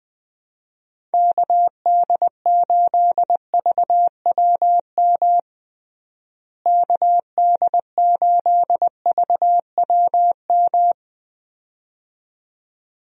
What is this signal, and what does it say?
Morse code, heard as a keyed tone, about 20 wpm, 700 Hz. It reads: KD8VWM KD8VWM
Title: USA Call Signs - 6 Characters - Repeated 2x - 20wpm